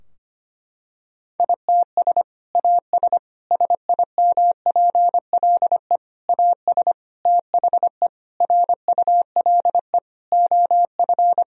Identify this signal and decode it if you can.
Morse code, heard as a keyed tone, about 25 wpm, 700 Hz. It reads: ITS AS SIMPLE AS THE RULE OF
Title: Sentences from Top 1000 Words - Review Part 2 - 25wpm